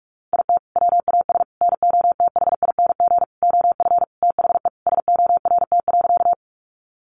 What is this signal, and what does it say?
Morse code, heard as a keyed tone, about 44 wpm, 700 Hz. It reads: IT WAS NOTHING OF THE SORT.
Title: Sentences from Top 1300 Words 45wpm